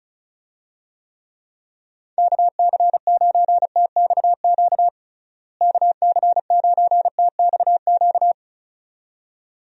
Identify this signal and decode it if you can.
Morse code, heard as a keyed tone, about 36 wpm, 700 Hz. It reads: KC9TXQ KC9TXQ
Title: USA Call Signs - 6 Characters - Repeated 2x - 35wpm